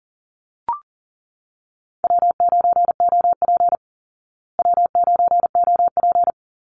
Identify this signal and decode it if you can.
Morse code, heard as a keyed tone, about 40 wpm, 700 Hz. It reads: W9OP W9OP
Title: Call Signs Repeated 2x - 40wpm